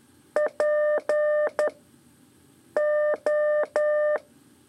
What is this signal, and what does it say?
Morse code, heard as a keyed tone, about 11 wpm, 600 Hz. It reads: P O
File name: 3 lezione CW 2 AIR